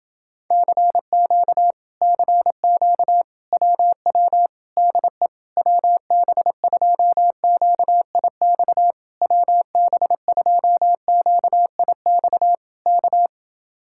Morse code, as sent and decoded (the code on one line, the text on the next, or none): -.-. --.- / -.-. --.- / .-- .-- / -.. / .-- -... ..--- --.- .. -..- / .-- -... ..--- --.- .. -..- / -.-
CQ CQ WW DE WB2QIX WB2QIX K